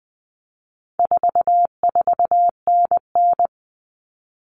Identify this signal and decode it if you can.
Morse code, heard as a keyed tone, about 20 wpm, 700 Hz. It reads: 44NN
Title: Common Cut Numbers - 4-Digit 20wpm